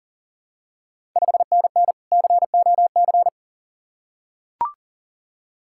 Morse code, heard as a keyed tone, about 42 wpm, 690 Hz. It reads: 5NN COC
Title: Ontario QSO Party Exchange 40wpm